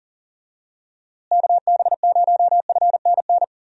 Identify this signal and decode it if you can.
Morse code, heard as a keyed tone, about 39 wpm, 700 Hz. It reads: KB0FNN
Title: USA Call Signs - 6 Characters - Repeated 2x - 40wpm